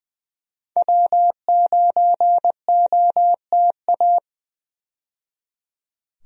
Morse code, heard as a keyed tone, about 20 wpm, 700 Hz. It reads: W9OTA
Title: USA Call Signs - 5 Characters - 20wpm